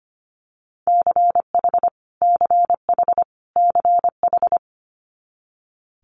Morse code, heard as a keyed tone, about 25 wpm, 700 Hz. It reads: CH CH CH
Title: Multi-letter Phonemes Repeated 3x 25wpm